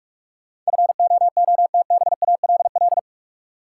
Morse code, heard as a keyed tone, about 46 wpm, 700 Hz. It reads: FOOTBALL
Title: Sentences from Top 1600 Words - New Word Drill Alt - 45wpm